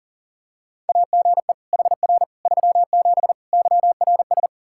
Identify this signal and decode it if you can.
Morse code, heard as a keyed tone, about 40 wpm, 700 Hz. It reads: AGE HR 37 YRS
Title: Practice QSO - Exchange 3 - More Info - 40wpm